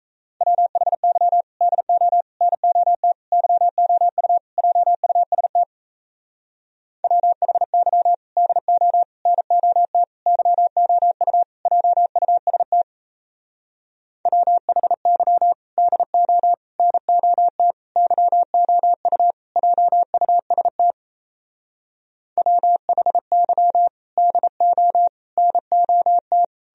Morse code, heard as a keyed tone, about 38 wpm, 700 Hz. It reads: WHY DO NOT YOU JUST WHY DO NOT YOU JUST WHY DO NOT YOU JUST WHY DO NOT